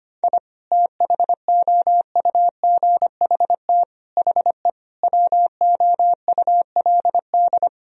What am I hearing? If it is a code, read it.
Morse code, heard as a keyed tone, about 25 wpm, 700 Hz. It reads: I THOUGHT HE WOULD
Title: Sentences from Top 1000 Words - Review Part 1 - 25wpm